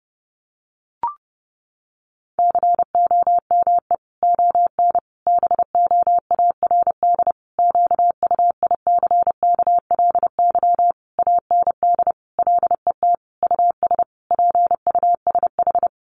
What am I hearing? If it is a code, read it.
Morse code, heard as a keyed tone, about 30 wpm, 700 Hz. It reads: COME ON BOARD QUICKLY AND LET US PUSH
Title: Sentences from Top 1300 Words 30wpm